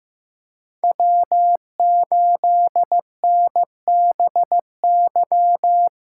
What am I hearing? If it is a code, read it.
Morse code, heard as a keyed tone, about 18 wpm, 700 Hz. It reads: W8NBY
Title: Speed Racing - USA 1x3 Call Signs 15wpm